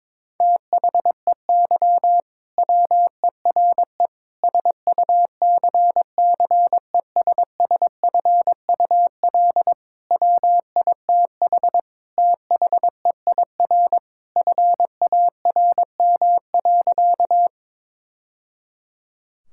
Morse code, heard as a keyed tone, about 22 wpm, 700 Hz. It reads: THEY WERE SUCCESSFUL WITH THEIR FARM.
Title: Sentences from Top 1000 Words - Review Part 6 - 22wpm